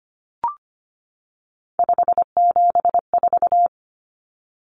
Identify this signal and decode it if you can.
Morse code, heard as a keyed tone, about 25 wpm, 700 Hz. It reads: 574
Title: Common Cut Numbers - 3-Digit 25wpm